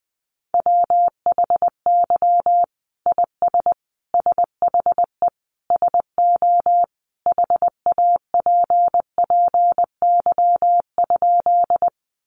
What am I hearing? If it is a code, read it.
Morse code, heard as a keyed tone, about 20 wpm, 700 Hz. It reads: WHY IS SHE SO HAPPY?
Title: Sentences from Top 1000 Words 20wpm